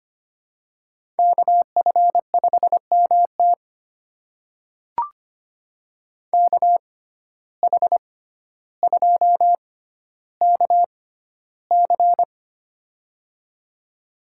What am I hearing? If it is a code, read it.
Morse code, heard as a keyed tone, about 25 wpm, spaced as for 13 wpm, 700 Hz. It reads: KF5MT KH2KC
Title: USA Call Signs - 5 Characters - 6x Character Spacing - 25wpm